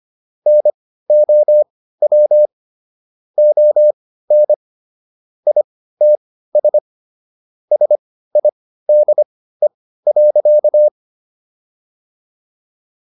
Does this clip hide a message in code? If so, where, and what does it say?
Yes: Morse code from 0.5 to 10.9 s, heard as a keyed tone, about 25 wpm, spaced as for 15 wpm, 600 Hz. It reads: NOW ON ITS SIDE.